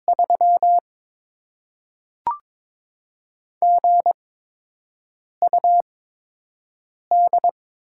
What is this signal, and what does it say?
Morse code, heard as a keyed tone, about 22 wpm, spaced as for 6 wpm, 700 Hz. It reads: 3 GUD
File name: Top 20 QSO Elements - 8x Character Spacing - 22wpm